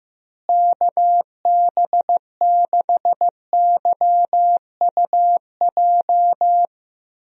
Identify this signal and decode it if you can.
Morse code, heard as a keyed tone, about 15 wpm, 700 Hz. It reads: KB6YUJ
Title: USA Call Signs - 6 Characters - Repeated 2x - 15wpm